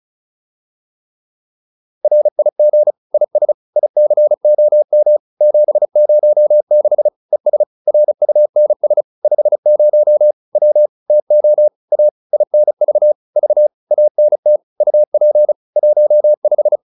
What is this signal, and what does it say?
Morse code, heard as a keyed tone, about 34 wpm, 600 Hz. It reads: RIG IS ICOM 706 ES RUNS 50 W TO A INV V ANT UP 15